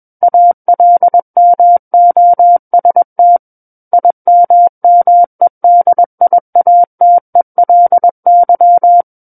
Morse code, heard as a keyed tone, about 21 wpm, 700 Hz. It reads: ALMOST IMMEDIATELY
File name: Book One / 12-Chapter 12 - 21 WPM